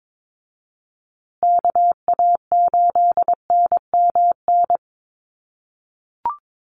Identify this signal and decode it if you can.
Morse code, heard as a keyed tone, about 22 wpm, 700 Hz. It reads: KA8NMN
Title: USA Call Signs Repeated 3x 22wpm